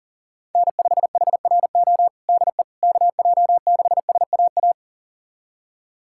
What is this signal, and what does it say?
Morse code, heard as a keyed tone, about 40 wpm, 700 Hz. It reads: N5HRO DE KJ6SAA
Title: USA Callsign DE Callsign 40wpm